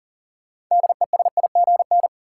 Morse code, heard as a keyed tone, about 40 wpm, 700 Hz. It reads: DESIGN
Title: Sentences from Top 800 Words - New Word Drill - 40wpm